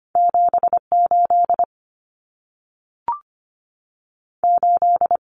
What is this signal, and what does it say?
Morse code, heard as a keyed tone, about 25 wpm, 700 Hz. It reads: 78 8
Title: Numbers - 2-Digit 25wpm